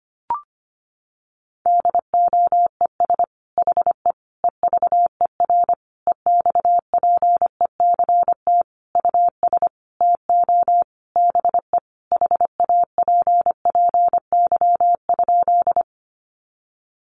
Morse code, heard as a keyed tone, about 25 wpm, 700 Hz. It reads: DOES HE EVER EXPECT US TO BE HAPPY?
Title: Sentences from Top 1100 Words 25wpm